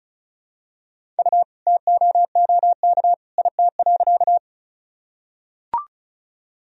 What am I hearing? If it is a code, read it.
Morse code, heard as a keyed tone, about 36 wpm, 700 Hz. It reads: U TOOK IT.
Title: Sentences from Top 300 Words and Abbreviations 35wpm